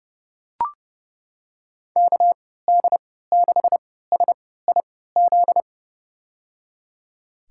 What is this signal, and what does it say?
Morse code, heard as a keyed tone, about 32 wpm, spaced as for 17 wpm, 700 Hz. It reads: KD6SIZ
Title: USA Call Signs - 6 Characters - 3x Character Spacing - 30wpm